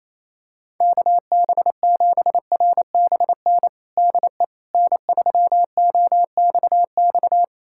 Morse code, heard as a keyed tone, about 28 wpm, 700 Hz. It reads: KB7RBN DE N3OXX
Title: USA Callsign DE Callsign 28wpm